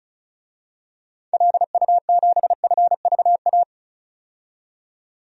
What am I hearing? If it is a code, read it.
Morse code, heard as a keyed tone, about 34 wpm, 700 Hz. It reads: LU7FVA
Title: DXCC-100 Call Signs - 6 Characters - 35wpm